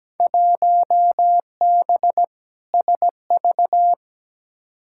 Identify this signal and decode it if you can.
Morse code, heard as a keyed tone, about 17 wpm, 700 Hz. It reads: 1B SV
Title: Field Day Exchanges Encore - 17wpm